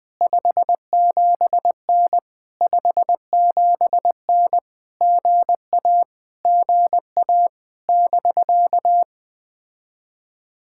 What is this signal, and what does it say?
Morse code, heard as a keyed tone, about 20 wpm, 700 Hz. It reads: 57N 57N GA GA <BK>
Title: POTA QSO Practice - 20wpm